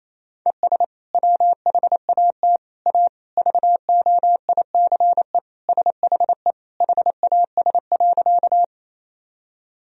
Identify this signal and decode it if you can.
Morse code, heard as a keyed tone, about 28 wpm, 700 Hz. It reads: ES WHAT A VOICE SHE HAS.
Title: Sentences from Top 500 Words and Abbreviations 28wpm